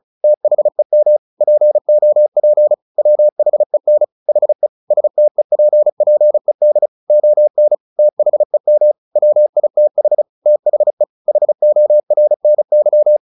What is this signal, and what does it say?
Morse code, heard as a keyed tone, about 34 wpm, 600 Hz. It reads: THEM POP WHEN HE STEPPED ON THEM WITH THE HORNY
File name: the old man and the sea / the old man and the sea 1 35wpm